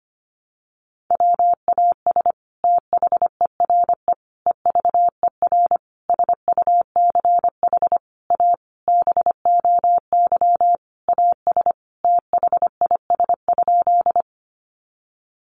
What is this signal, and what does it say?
Morse code, heard as a keyed tone, about 25 wpm, 700 Hz. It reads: WAS THERE EVER SUCH A BOY AS THIS?
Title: Sentences from Top 400 Words 25wpm